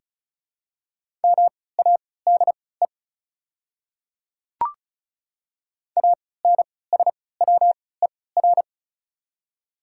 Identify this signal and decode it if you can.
Morse code, heard as a keyed tone, about 36 wpm, spaced as for 20 wpm, 690 Hz. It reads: MADE ANSWER
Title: 3x Character Spacing - Top 300 Words - 35wpm